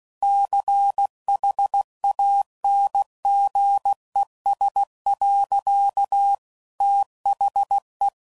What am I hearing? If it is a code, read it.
Morse code, heard as a keyed tone, about 16 wpm, 800 Hz. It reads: CHANGES. THE
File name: CW Code Groups @15WPM / Technical Subjects / XC